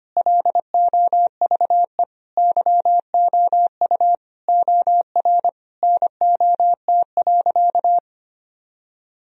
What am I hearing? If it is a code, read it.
Morse code, heard as a keyed tone, about 25 wpm, 700 Hz. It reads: LOVE YOU OR NOT.
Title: Sentences from Top 1000 Words - Review Part 3 - 25wpm